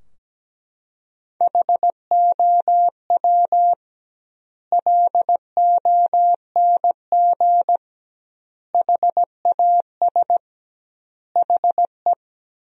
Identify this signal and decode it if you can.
Morse code, heard as a keyed tone, about 17 wpm, 700 Hz. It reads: HOW LONG HAS HE
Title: 2x Word-Spacing - Sentences from Top 100 Words - 17wpm